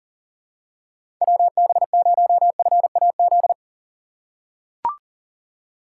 Morse code, heard as a keyed tone, about 37 wpm, 700 Hz. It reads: WB0FAZ
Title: USA Call Signs - 6 Characters - 4x Character Spacing - 40wpm